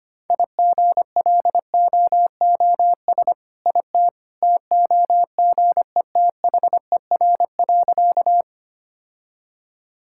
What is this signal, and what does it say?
Morse code, heard as a keyed tone, about 25 wpm, 700 Hz. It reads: IGLOOS IT TOGETHER.
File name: practice / 25wpm / All-Age Appropriate Jokes - 25wpm